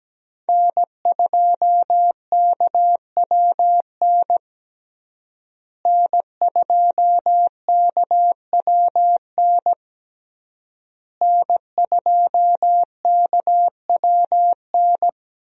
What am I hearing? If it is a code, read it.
Morse code, heard as a keyed tone, about 17 wpm, 700 Hz. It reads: N2KWN N2KWN N2KWN
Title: USA Call Signs - 5 Characters - Repeated 3x - 17wpm